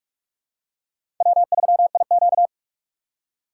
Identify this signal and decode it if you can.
Morse code, heard as a keyed tone, about 41 wpm, 700 Hz. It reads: W3IQ
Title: USA Call Signs - 4x Character Spacing - 45wpm